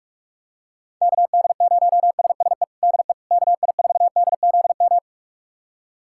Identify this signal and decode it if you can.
Morse code, heard as a keyed tone, about 46 wpm, 700 Hz. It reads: KD0SSE DE KI4DZM
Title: USA Callsign DE Callsign 45wpm